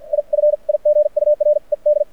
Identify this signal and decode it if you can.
Morse code, heard as a keyed tone, about 52 wpm, spaced as for 42 wpm, 590 Hz. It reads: I FIGURED
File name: archived / CRSnet